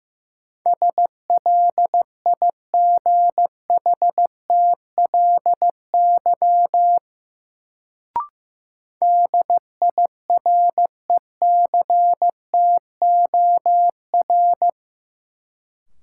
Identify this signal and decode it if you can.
Morse code, heard as a keyed tone, about 15 wpm, 700 Hz. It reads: SLIGHTLY DIRECTOR
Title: Sentences from Top 1300 Words - New Word Drill Alt - 15wpm